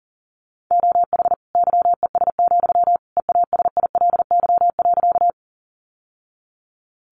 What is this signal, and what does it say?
Morse code, heard as a keyed tone, about 40 wpm, 700 Hz. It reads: OH YES, EASILY.
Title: Sentences from Top 900 Words 40wpm